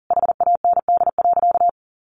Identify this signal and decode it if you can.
Morse code, heard as a keyed tone, about 40 wpm, 700 Hz. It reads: HAND.
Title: Sentences from Top 1300 Words 40wpm